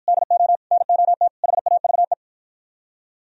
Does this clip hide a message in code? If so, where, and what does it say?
Yes: Morse code from 0.1 to 2.1 s, heard as a keyed tone, about 50 wpm, 700 Hz. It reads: DO NOT HAVE